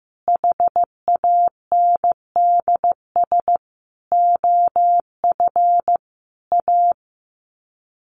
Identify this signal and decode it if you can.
Morse code, heard as a keyed tone, about 15 wpm, 700 Hz. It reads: HANDS OF A
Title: Sets of 5 Words Encore 15wpm